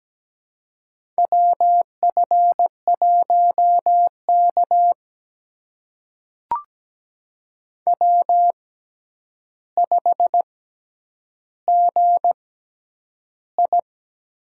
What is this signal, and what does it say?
Morse code, heard as a keyed tone, about 17 wpm, spaced as for 9 wpm, 700 Hz. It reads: WF1K W5GI
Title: USA Call Signs - 4 Characters - 6x Character Spacing - 17wpm